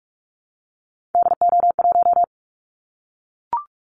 Morse code, heard as a keyed tone, about 44 wpm, 700 Hz. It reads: DO1
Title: Callsign Prefixes for Top 50 DXCC - 45wpm